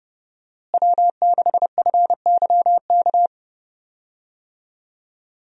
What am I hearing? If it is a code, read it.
Morse code, heard as a keyed tone, about 30 wpm, 700 Hz. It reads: W6FYK
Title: USA Call Signs - 5 Characters - Repeated 2x - 30wpm